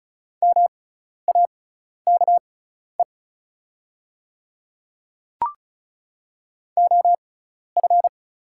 Morse code, heard as a keyed tone, about 36 wpm, spaced as for 12 wpm, 690 Hz. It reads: MAKE OF
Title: 6x Character Spacing - Top 200 Words - 35wpm